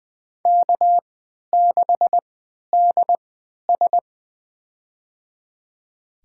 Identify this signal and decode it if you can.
Morse code, heard as a keyed tone, about 20 wpm, spaced as for 11 wpm, 700 Hz. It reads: K6DS
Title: USA Call Signs - 4 Characters - 3x Character Spacing - 20wpm